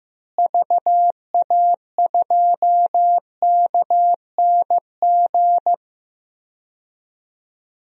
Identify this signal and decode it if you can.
Morse code, heard as a keyed tone, about 15 wpm, 700 Hz. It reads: VA2KNG
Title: DXCC-100 Call Signs - 6 Characters - Repeated 2x - 15wpm